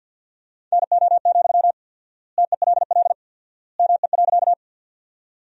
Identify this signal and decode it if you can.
Morse code, heard as a keyed tone, about 50 wpm, 700 Hz. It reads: NO, TELL ME.
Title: 4x Word-Spacing - Sentences from Top 200 Words - 50wpm